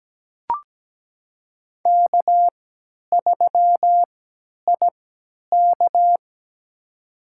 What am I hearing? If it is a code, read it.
Morse code, heard as a keyed tone, about 17 wpm, spaced as for 10 wpm, 700 Hz. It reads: K3IK
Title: USA Call Signs - 4 Characters - 3x Character Spacing - 17wpm